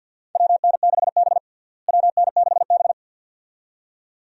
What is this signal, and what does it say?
Morse code, heard as a keyed tone, about 52 wpm, 700 Hz. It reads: WN6B WN6B
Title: USA Call Signs - 4 Characters - Repeated 2x - 50wpm